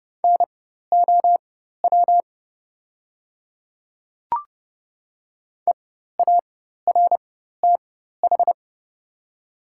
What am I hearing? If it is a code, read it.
Morse code, heard as a keyed tone, about 30 wpm, spaced as for 14 wpm, 700 Hz. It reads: NOW EARTH
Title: 4x Character Spacing - Top 300 Words - 30wpm